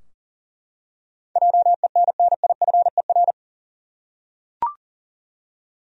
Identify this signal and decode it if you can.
Morse code, heard as a keyed tone, about 40 wpm, 700 Hz. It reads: JENNIFER
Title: USA Common Names - 40wpm